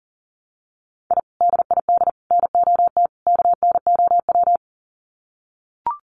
Morse code, heard as a keyed tone, about 40 wpm, 700 Hz. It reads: I DID NOT KNOW
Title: Sets of 4 Words 40wpm